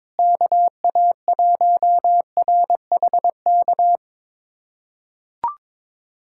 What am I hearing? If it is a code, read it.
Morse code, heard as a keyed tone, about 21 wpm, 700 Hz. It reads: KA1RHK
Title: USA Callsign DE Callsign - 6 Characters - 22wpm